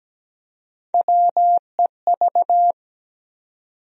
Morse code, heard as a keyed tone, about 17 wpm, 700 Hz. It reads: WEV
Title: USA Call Sign Postfixes - 3 Letters 17wpm